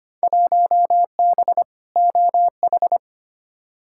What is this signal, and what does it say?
Morse code, heard as a keyed tone, about 25 wpm, 700 Hz. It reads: 1B OH
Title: Field Day Exchanges - 25wpm